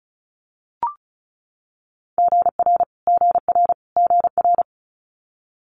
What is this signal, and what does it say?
Morse code, heard as a keyed tone, about 36 wpm, 700 Hz. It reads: GR GR GR
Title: Multi-letter Phonemes Repeated 3x 35wpm